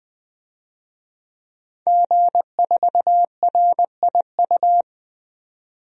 Morse code, heard as a keyed tone, about 20 wpm, 700 Hz. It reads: G4RIU
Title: DXCC-100 Call Signs - 5 Characters - 20wpm